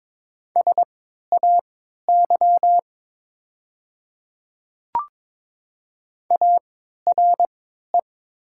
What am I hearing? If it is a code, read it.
Morse code, heard as a keyed tone, about 22 wpm, spaced as for 12 wpm, 690 Hz. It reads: SAY ARE
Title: 3x Character Spacing - Top 500 Words - 22wpm